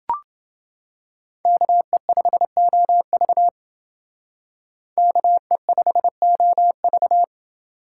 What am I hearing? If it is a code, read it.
Morse code, heard as a keyed tone, about 28 wpm, 700 Hz. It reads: KE5OV KE5OV